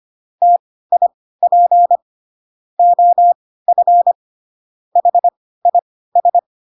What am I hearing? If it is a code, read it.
Morse code, heard as a keyed tone, about 25 wpm, spaced as for 16 wpm, 700 Hz. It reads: TIP OF HIS
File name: Pinocchio / Pinocchio ch1 10 25 ik1hge